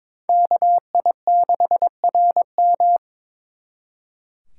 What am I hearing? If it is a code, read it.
Morse code, heard as a keyed tone, about 26 wpm, 700 Hz. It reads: KI6RM
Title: Speed Racing - USA 2x2 Call Signs 22wpm